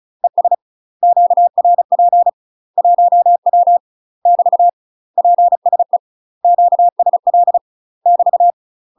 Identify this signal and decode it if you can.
Morse code, heard as a keyed tone, about 36 wpm, 700 Hz. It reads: ES QRP 1W <BT> PSE QSL <BT>
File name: UK MASTER QSOs TRAINING / 35WPM / UK MASTER QSOs 0040